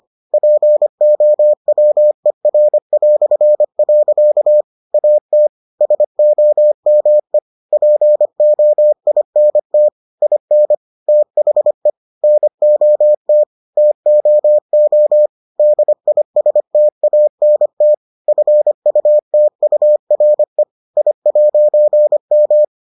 Morse code, heard as a keyed tone, about 25 wpm, 600 Hz. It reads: POWER". AT SOME POINT IN THE NOT TOO DISTANT FUTURE I'M